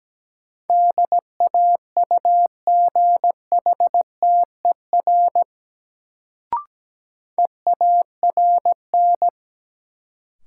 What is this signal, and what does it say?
Morse code, heard as a keyed tone, about 17 wpm, 700 Hz. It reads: DAUGHTER EARN
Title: Sentences from Top 1500 Words - New Word Drill - 17wpm